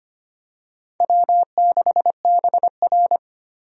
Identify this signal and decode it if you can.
Morse code, heard as a keyed tone, about 25 wpm, 700 Hz. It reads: W6BR
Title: USA Call Signs - 4 Characters - 25wpm